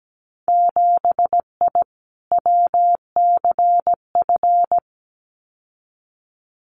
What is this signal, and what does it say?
Morse code, heard as a keyed tone, about 17 wpm, 700 Hz. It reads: 7I WCF
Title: Winter Field Day Exchanges Encore 17wpm